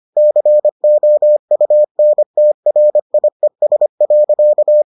Morse code, heard as a keyed tone, about 25 wpm, 600 Hz. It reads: COUNTRIES.